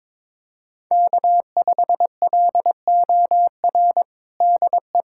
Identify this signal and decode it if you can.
Morse code, heard as a keyed tone, about 22 wpm, 700 Hz. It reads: K5LOR DE
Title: USA Callsign DE Callsign 22wpm